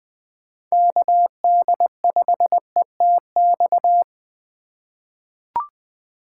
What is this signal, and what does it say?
Morse code, heard as a keyed tone, about 21 wpm, 700 Hz. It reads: KD5ETX
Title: USA Call Signs - 2x Character Spacing - 20wpm